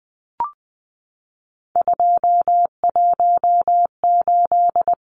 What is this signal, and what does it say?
Morse code, heard as a keyed tone, about 20 wpm, 700 Hz. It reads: 218
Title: Numbers - 3-Digit 20wpm